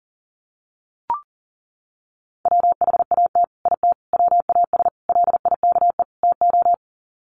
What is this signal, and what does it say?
Morse code, heard as a keyed tone, about 40 wpm, 700 Hz. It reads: WHAT IT WAS LIKE TO
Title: Sets of 5 Words Encore 40wpm